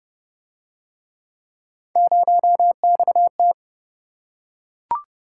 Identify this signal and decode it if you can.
Morse code, heard as a keyed tone, about 28 wpm, 700 Hz. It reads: 0XT